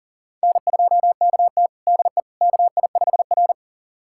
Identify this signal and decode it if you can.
Morse code, heard as a keyed tone, about 40 wpm, 700 Hz. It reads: N2KT DE KI5R
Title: USA Callsign DE Callsign - 4 Characters - 40wpm